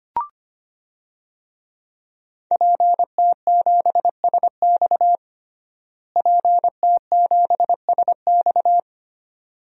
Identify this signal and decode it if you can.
Morse code, heard as a keyed tone, about 25 wpm, 700 Hz. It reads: PT7SX PT7SX
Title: DXCC-100 Call Signs - 5 Characters - Repeated 2x - 25wpm